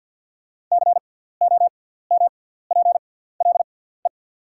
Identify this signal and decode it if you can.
Morse code, heard as a keyed tone, about 48 wpm, spaced as for 17 wpm, 690 Hz. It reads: COMPLE
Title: 6x Character Spacing - Top 500 Words - 50wpm